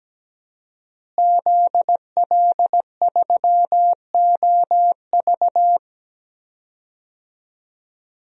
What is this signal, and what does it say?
Morse code, heard as a keyed tone, about 17 wpm, 700 Hz. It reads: ZL3OV
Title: DXCC-100 Call Signs - 5 Characters - Repeated 3x - 17wpm